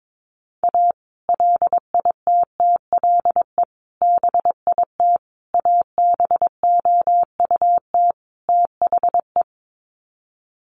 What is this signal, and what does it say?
Morse code, heard as a keyed tone, about 22 wpm, 700 Hz. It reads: A LITTLE BIT ABOUT THE
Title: Sets of 5 Words Encore2 22wpm